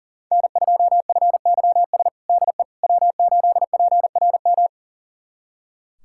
Morse code, heard as a keyed tone, about 40 wpm, 690 Hz. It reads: N2FYS DE W8PRM
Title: USA Callsign DE Callsign - 5 Characters - 40wpm